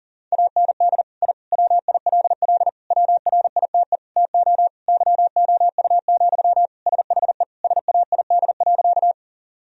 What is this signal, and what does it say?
Morse code, heard as a keyed tone, about 40 wpm, 700 Hz. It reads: AND I WILL WRITE TO YOU, SHE SAID.
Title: Sentences from Top 1000 Words - Review Part 1 - 40wpm